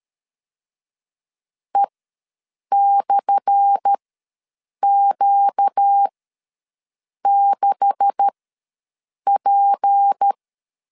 Morse code, heard as a keyed tone, about 13 wpm, spaced as for 7 wpm, 800 Hz. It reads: E/Q6P
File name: LEZIONI / 10 SIGNS / 10B INT ws 12wpm 3527